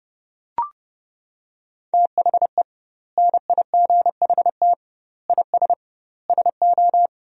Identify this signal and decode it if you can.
Morse code, heard as a keyed tone, about 31 wpm, 700 Hz. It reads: THE NIGHT IS SO
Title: 2x Word-Spacing - Sentences from Top 500 Words - 30wpm